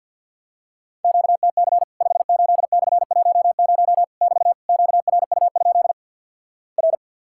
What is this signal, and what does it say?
Morse code, heard as a keyed tone, about 50 wpm, 700 Hz. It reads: QTC 58/10 <BT> QRU? R
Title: Worked All Europe DX Contest - QTC Exchange - 50wpm